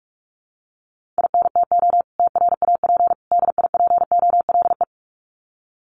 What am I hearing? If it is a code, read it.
Morse code, heard as a keyed tone, about 44 wpm, 700 Hz. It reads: INTO TRAP DIPOLE
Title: Common QSO Phrases 45wpm